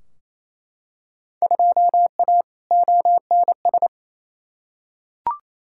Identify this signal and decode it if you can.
Morse code, heard as a keyed tone, about 28 wpm, 700 Hz. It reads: 2A ONS
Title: Field Day Exchanges Encore - 28wpm